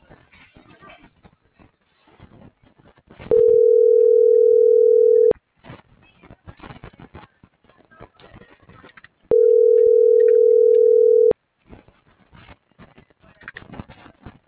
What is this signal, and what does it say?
Ringback tone (North America), 440 + 480 Hz.
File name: CENTRAL MAYAB / out fernanda.franco